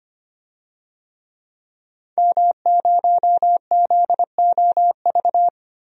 Morse code, heard as a keyed tone, about 25 wpm, 690 Hz. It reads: M0ZOV
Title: DXCC-100 Call Signs - 5 Characters - 25wpm